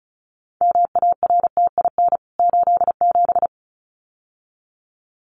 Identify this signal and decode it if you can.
Morse code, heard as a keyed tone, about 34 wpm, 700 Hz. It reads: MARTIN 87
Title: CWT Exchange 35wpm